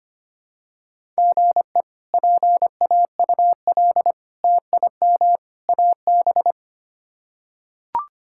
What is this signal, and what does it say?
Morse code, heard as a keyed tone, about 25 wpm, 700 Hz. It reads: GE PAUL TIM AB
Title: K1USN SST QSO Practice - 25wpm